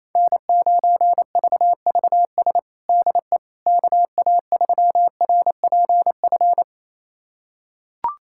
Morse code, heard as a keyed tone, about 28 wpm, 700 Hz. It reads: N9VVS DE KA3RPF
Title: USA Callsign DE Callsign 28wpm